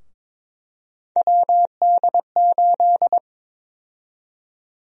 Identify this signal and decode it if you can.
Morse code, heard as a keyed tone, about 22 wpm, 700 Hz. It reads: WD8
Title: USA Call Sign Prefixes - 22wpm